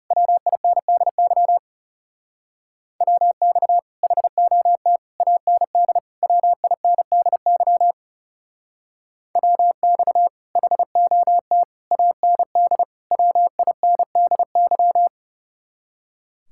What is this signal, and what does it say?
Morse code, heard as a keyed tone, about 37 wpm, 700 Hz. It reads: WINDY WX HOT AND WINDY WX HOT AND WINDY
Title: Speed-Racing Common QSO Phrases - 30wpm